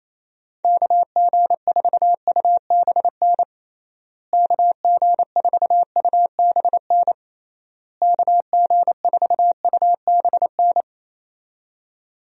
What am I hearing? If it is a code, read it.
Morse code, heard as a keyed tone, about 28 wpm, 700 Hz. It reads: KG4UBN KG4UBN KG4UBN
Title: USA Call Signs - 6 Characters - Repeated 3x - 28wpm